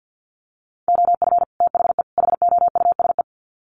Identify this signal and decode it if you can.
Morse code, heard as a keyed tone, about 50 wpm, 700 Hz. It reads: OF THE HOUSE
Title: Sets of 3 Words 50wpm